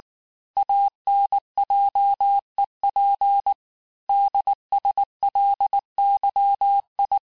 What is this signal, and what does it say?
Morse code, heard as a keyed tone, about 19 wpm, 800 Hz. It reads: ANJEP DSLYI
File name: cours-du-mercredi / audios / cours-869-texte-CHIFFRE-vitesse-18-MPM